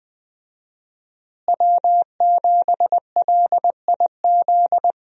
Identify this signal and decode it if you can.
Morse code, heard as a keyed tone, about 20 wpm, 700 Hz. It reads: W7LIZ
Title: USA Call Signs - 5 Characters - Repeated 2x - 20wpm